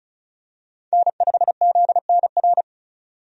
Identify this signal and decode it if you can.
Morse code, heard as a keyed tone, about 36 wpm, 700 Hz. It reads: N5ZNR
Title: USA Call Signs - 5 Characters - 35wpm